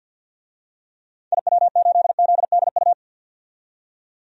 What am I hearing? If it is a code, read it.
Morse code, heard as a keyed tone, about 52 wpm, 700 Hz. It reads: IW8ZDU
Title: DXCC-100 Call Signs - 6 Characters - 50wpm